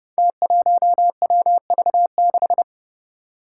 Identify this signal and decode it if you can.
Morse code, heard as a keyed tone, about 30 wpm, 700 Hz. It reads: T1WV6
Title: Five Challenging Letter-Number Combinations 30wpm